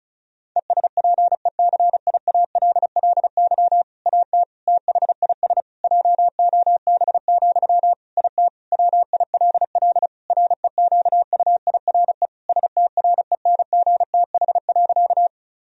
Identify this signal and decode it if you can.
Morse code, heard as a keyed tone, about 36 wpm, 700 Hz. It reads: ESPECIALLY AT THIS JOB, IT WILL REQUIRE STRENGTH.
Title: Sentences from Top 1000 Words - Review Part 4 - 35wpm